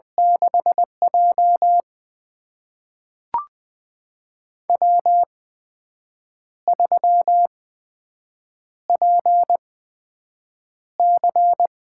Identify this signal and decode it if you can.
Morse code, heard as a keyed tone, about 20 wpm, spaced as for 7 wpm, 700 Hz. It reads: 6J W3PC
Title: USA Call Signs - 4 Characters - 8x Character Spacing - 20wpm